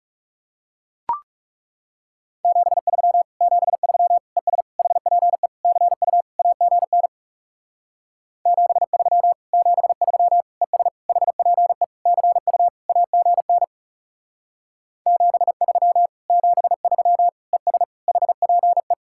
Morse code, heard as a keyed tone, about 36 wpm, 700 Hz. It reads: 73 73 ES HPE CU AGN 73 73 ES HPE CU AGN 73 73 ES HPE
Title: Speed-Racing Common QSO Phrases - 30wpm